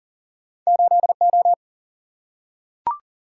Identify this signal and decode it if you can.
Morse code, heard as a keyed tone, about 40 wpm, 700 Hz. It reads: 8 O
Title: Single-Letter Single-Number 40wpm